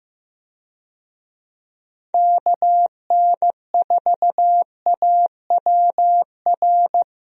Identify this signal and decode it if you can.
Morse code, heard as a keyed tone, about 15 wpm, 700 Hz. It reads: KN4AWR
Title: USA Call Signs - 6 Characters - Repeated 2x - 15wpm